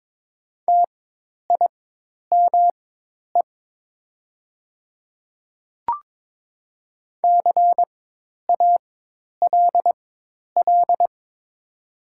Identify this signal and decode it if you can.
Morse code, heard as a keyed tone, about 22 wpm, spaced as for 10 wpm, 690 Hz. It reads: TIME CALL
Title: 4x Character Spacing - Top 200 Words - 22wpm